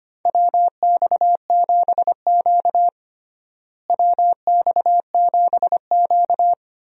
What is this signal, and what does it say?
Morse code, heard as a keyed tone, about 25 wpm, 700 Hz. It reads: WX7Q WX7Q
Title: USA Call Signs - 4 Characters - Repeated 3x - 25wpm